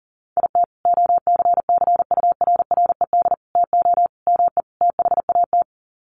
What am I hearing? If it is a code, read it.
Morse code, heard as a keyed tone, about 40 wpm, 700 Hz. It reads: IT OCCURRED TO ME THAT
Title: Sets of 5 Words Encore 40wpm